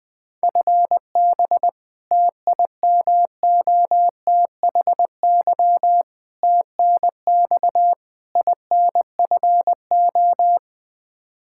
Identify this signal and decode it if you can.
Morse code, heard as a keyed tone, about 20 wpm, 700 Hz. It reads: FB TIMOTHY TNX INFO
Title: K1USN SST QSO Practice 20wpm